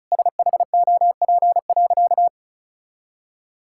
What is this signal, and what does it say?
Morse code, heard as a keyed tone, about 34 wpm, 700 Hz. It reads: SHOP.
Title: Sentences from Top 1600 Words 35wpm